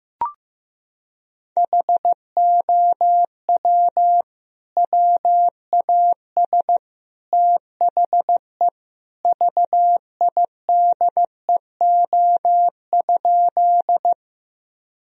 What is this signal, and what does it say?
Morse code, heard as a keyed tone, about 15 wpm, 700 Hz. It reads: HOW WAS THE VIDEO?
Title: Sentences from Top 1000 Words - Review Part 4 - 15wpm